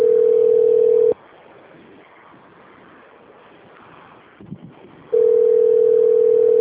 Ringback tone (North America), 440 + 480 Hz.
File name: rg-1600-unknown- - - .21557